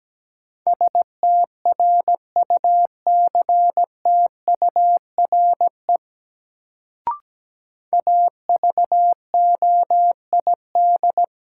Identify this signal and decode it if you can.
Morse code, heard as a keyed tone, about 17 wpm, 700 Hz. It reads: STRUCTURE AVOID
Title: Sentences from Top 1000 Words - Review Part 6 - New Word Drill - 17wpm